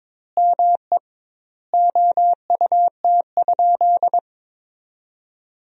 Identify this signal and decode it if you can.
Morse code, heard as a keyed tone, about 22 wpm, 700 Hz. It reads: ME OUT?
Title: 2x Word-Spacing - Sentences from Top 200 Words - 22wpm